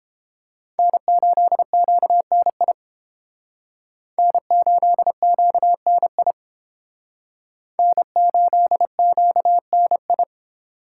Morse code, heard as a keyed tone, about 28 wpm, 700 Hz. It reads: N8QNI N8QNI N8QNI